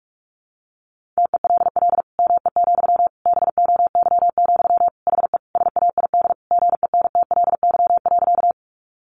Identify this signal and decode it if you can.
Morse code, heard as a keyed tone, about 44 wpm, 700 Hz. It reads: TELL ME, BOY, HE SAID GENTLY.
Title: Sentences from Top 1500 Words 45wpm